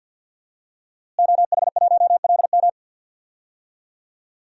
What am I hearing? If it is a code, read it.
Morse code, heard as a keyed tone, about 52 wpm, 700 Hz. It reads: OH1LM
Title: DXCC-100 Call Signs - 5 Characters - Repeated 2x - 50wpm